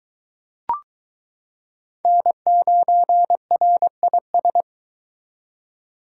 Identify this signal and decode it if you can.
Morse code, heard as a keyed tone, about 23 wpm, 700 Hz. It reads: N9RIS